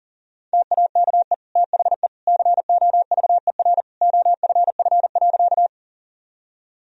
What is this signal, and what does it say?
Morse code, heard as a keyed tone, about 40 wpm, 700 Hz. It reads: TAKE THE COVER OFF.
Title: Sentences from Top 1000 Words - Review Part 1 - 40wpm